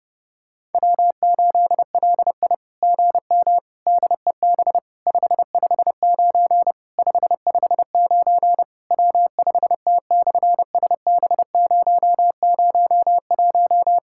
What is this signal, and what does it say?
Morse code, heard as a keyed tone, about 30 wpm, 700 Hz. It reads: W8LI GM DEB 559 559 W5T/SB001
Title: SOTA QSO Practice 30wpm